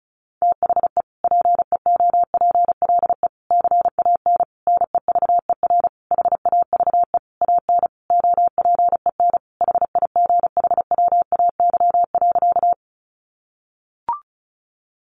Morse code, heard as a keyed tone, about 34 wpm, 700 Hz. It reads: THE PEOPLE CAN NEVER HAVE AN OPEN HIGHWAY.
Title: Sentences from Top 1400 Words 35wpm